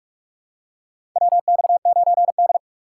Morse code, heard as a keyed tone, about 46 wpm, 700 Hz. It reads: WX9D
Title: USA Call Signs - 4 Characters - 3x Character Spacing - 45wpm